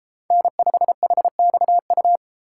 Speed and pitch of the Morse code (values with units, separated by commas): 32 wpm, 700 Hz